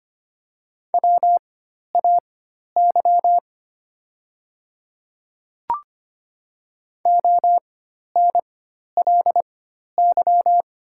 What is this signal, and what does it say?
Morse code, heard as a keyed tone, about 25 wpm, spaced as for 12 wpm, 700 Hz. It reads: WAY ONLY
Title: 4x Character Spacing - Top 200 Words - 25wpm